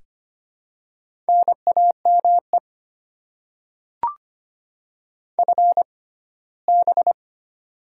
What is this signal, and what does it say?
Morse code, heard as a keyed tone, about 25 wpm, spaced as for 17 wpm, 700 Hz. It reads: NAME FB
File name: Top 20 QSO Elements - 6x Character Spacing - 25wpm